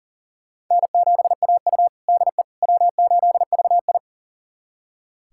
Morse code, heard as a keyed tone, about 40 wpm, 700 Hz. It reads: N7AU DE W8VI
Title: USA Callsign DE Callsign - 4 Characters - 40wpm